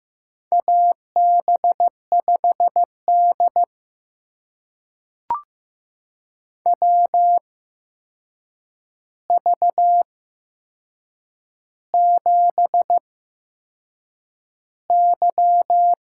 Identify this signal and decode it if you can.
Morse code, heard as a keyed tone, about 15 wpm, spaced as for 6 wpm, 700 Hz. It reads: AB5D WV7Y
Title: USA Call Signs - 4 Characters - 8x Character Spacing - 15wpm